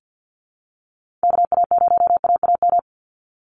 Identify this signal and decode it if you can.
Morse code, heard as a keyed tone, about 50 wpm, 700 Hz. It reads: KA0AAM
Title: Call Signs Repeated 2x - 50wpm